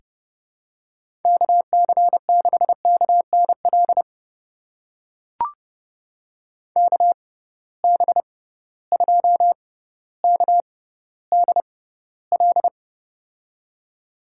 Morse code, heard as a keyed tone, about 30 wpm, spaced as for 15 wpm, 700 Hz. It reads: KC6KNL KB2KDL
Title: USA Call Signs - 6 Characters - 6x Character Spacing - 30wpm